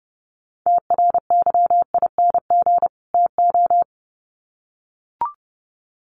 Morse code, heard as a keyed tone, about 30 wpm, 700 Hz. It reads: TRYING TO
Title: Sets of 2 Words 30wpm